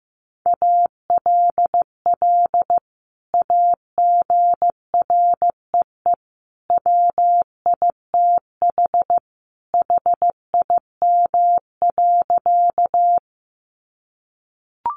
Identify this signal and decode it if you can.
Morse code, heard as a keyed tone, about 14 wpm, 700 Hz. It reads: ALL AGREE WITH HIM.
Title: Sentences from Top 1100 Words 15wpm